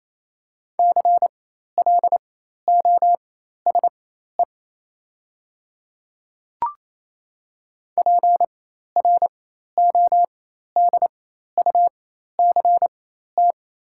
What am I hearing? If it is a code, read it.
Morse code, heard as a keyed tone, about 28 wpm, spaced as for 13 wpm, 700 Hz. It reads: CLOSE PRODUCT
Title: 4x Character Spacing - Top 500 Words - 28wpm